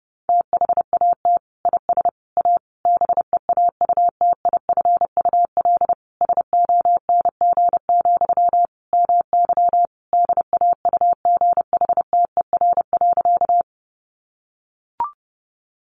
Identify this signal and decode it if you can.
Morse code, heard as a keyed tone, about 30 wpm, 700 Hz. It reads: THAT IS A BEAUTIFUL SONG, MY DAUGHTER.
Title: Sentences from Top 1500 Words 30wpm